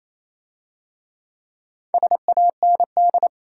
Morse code, heard as a keyed tone, about 27 wpm, 700 Hz. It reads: SAND
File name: Easy Words - Speed-Racing with Hints - 22wpm